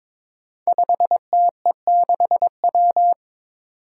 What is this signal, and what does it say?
Morse code, heard as a keyed tone, about 22 wpm, 700 Hz. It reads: 5TE6W
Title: Five Challenging Letter-Number Combinations 22wpm